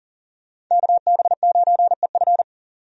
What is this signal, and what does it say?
Morse code, heard as a keyed tone, about 40 wpm, 700 Hz. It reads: KB9EF
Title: USA Call Signs - 5 Characters - 40wpm